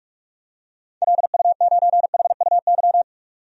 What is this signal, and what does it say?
Morse code, heard as a keyed tone, about 44 wpm, 700 Hz. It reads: LU9HUY
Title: DXCC-100 Call Signs - 6 Characters - 45wpm